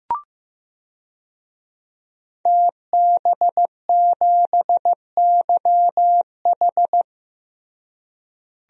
Morse code, heard as a keyed tone, about 15 wpm, 700 Hz. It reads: TB7YH
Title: DXCC-100 Call Signs - 5 Characters - Repeated 3x - 15wpm